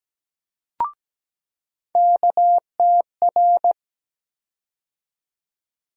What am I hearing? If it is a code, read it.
Morse code, heard as a keyed tone, about 17 wpm, 700 Hz. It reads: KTR
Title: USA Call Sign Postfixes - 3 Letters 17wpm